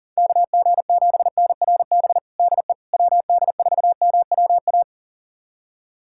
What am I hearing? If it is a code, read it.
Morse code, heard as a keyed tone, about 40 wpm, 700 Hz. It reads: KG7NRB DE WD4MWA
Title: USA Callsign DE Callsign 40wpm